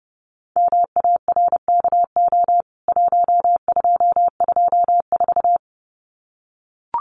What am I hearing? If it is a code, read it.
Morse code, heard as a keyed tone, about 30 wpm, 700 Hz. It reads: MARKO 1224
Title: CWT Exchange 30wpm